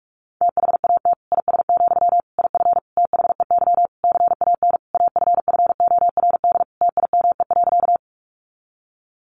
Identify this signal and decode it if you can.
Morse code, heard as a keyed tone, about 44 wpm, 700 Hz. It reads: THAT IS, IF THEY CAN AFFORD TIME.
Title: Sentences from Top 1100 Words 45wpm